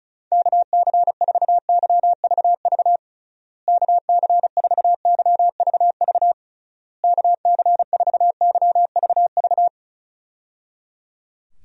Morse code, heard as a keyed tone, about 34 wpm, 700 Hz. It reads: KC4YVV KC4YVV KC4YVV
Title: USA Call Signs - 6 Characters - Repeated 3x - 35wpm